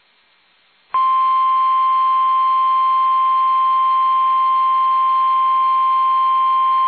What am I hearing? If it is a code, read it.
Emergency Alert System (SAME) tones (the 1050 Hz attention tone). It is heard from 1.2 to 6.2 s.
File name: eas / KUSO / KUSO